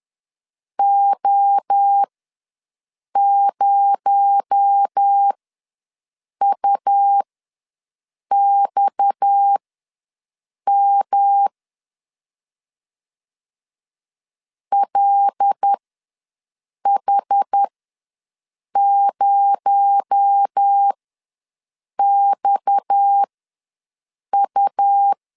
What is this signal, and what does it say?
Morse code, heard as a keyed tone, about 10 wpm, spaced as for 6 wpm, 800 Hz. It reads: O0UXM LH0XU
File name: LEZIONI / 6 PKX0 / 6 PKX0 ws 10wpm 4220